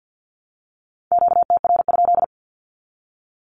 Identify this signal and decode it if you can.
Morse code, heard as a keyed tone, about 50 wpm, 700 Hz. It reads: QTR?
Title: Q-Codes Repeated 3x 50wpm